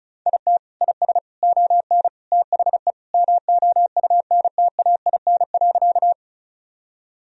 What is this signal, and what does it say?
Morse code, heard as a keyed tone, about 34 wpm, 700 Hz. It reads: IT IS ON THE MOUNTAIN.
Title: Sentences from Top 1000 Words - Review Part 1 - 35wpm